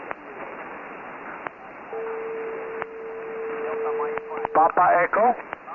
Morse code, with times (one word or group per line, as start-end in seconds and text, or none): none